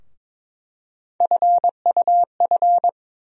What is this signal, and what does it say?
Morse code, heard as a keyed tone, about 21 wpm, 700 Hz. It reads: FUF
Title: USA Call Sign Postfixes - 3 Letters 22wpm